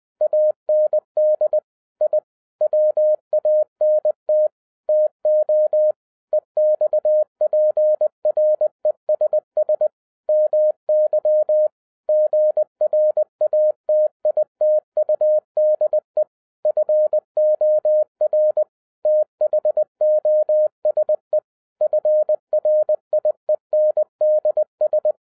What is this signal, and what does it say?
Morse code, heard as a keyed tone, about 20 wpm, 610 Hz. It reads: AND I WANT TO EXPRESS MY GRATITUDE FOR THOSE FRIENDS
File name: 2011-07-mp3 / n48 JOHN K3TN 20wpm